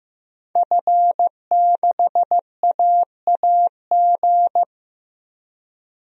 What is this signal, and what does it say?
Morse code, heard as a keyed tone, about 15 wpm, 700 Hz. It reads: F6AAG
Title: DXCC-100 Call Signs - 5 Characters - 15wpm